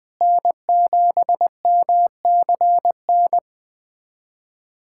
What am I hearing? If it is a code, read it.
Morse code, heard as a keyed tone, about 20 wpm, 700 Hz. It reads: N7MCN
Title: USA Callsign DE Callsign - 5 Characters - 20wpm